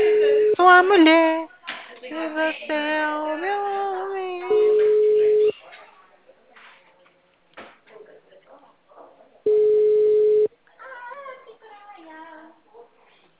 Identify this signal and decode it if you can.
Ringback tone (Europe, 425 Hz), with speech.
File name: out- -2006- - - .16896